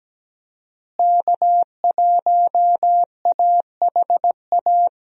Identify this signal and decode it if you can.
Morse code, heard as a keyed tone, about 17 wpm, 700 Hz. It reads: K1AHA
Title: USA Call Signs - 5 Characters - Repeated 3x - 17wpm